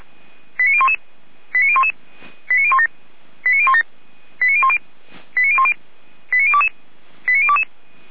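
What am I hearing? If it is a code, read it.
Five-tone selective call (ZVEI), heard as quick runs of five tones, tones of about 70 ms. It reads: 89011 89011 89017 89017 89010 89010 89022 89022
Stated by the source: ZVEI five-tone selective call 89001 received via SDR, hosted by feuerwehr-orlamuende fire brigade.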